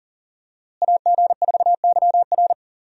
Morse code, heard as a keyed tone, about 40 wpm, 700 Hz. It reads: AG4YR
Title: USA Call Signs - 5 Characters - Repeated 2x - 40wpm